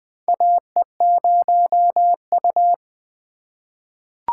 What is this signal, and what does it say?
Morse code, heard as a keyed tone, about 20 wpm, 700 Hz. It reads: AE0U
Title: USA Call Signs - 4 Characters - 4x Character Spacing - 20wpm